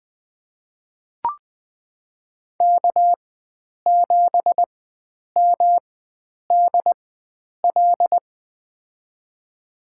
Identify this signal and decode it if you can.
Morse code, heard as a keyed tone, about 21 wpm, spaced as for 9 wpm, 700 Hz. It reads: K7MDL
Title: USA Call Signs - 5 Characters - 4x Character Spacing - 20wpm